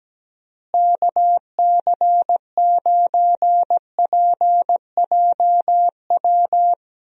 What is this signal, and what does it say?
Morse code, heard as a keyed tone, about 17 wpm, 700 Hz. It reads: KC9PJW
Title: USA Call Signs - 6 Characters - Repeated 3x - 17wpm